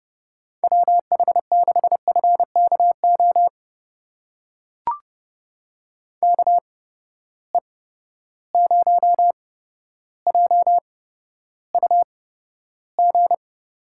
Morse code, heard as a keyed tone, about 30 wpm, spaced as for 13 wpm, 700 Hz. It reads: WH6FKO KE0JUG
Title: USA Call Signs - 6 Characters - 8x Character Spacing - 30wpm